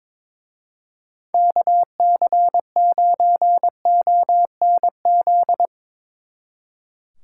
Morse code, heard as a keyed tone, about 24 wpm, 700 Hz. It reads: KC9ONZ
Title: Speed Racing - USA 2x3 Call Signs 22wpm